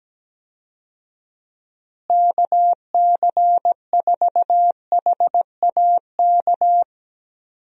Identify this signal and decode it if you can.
Morse code, heard as a keyed tone, about 17 wpm, 700 Hz. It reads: KC4HAK
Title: USA Call Signs - 6 Characters - Repeated 2x - 17wpm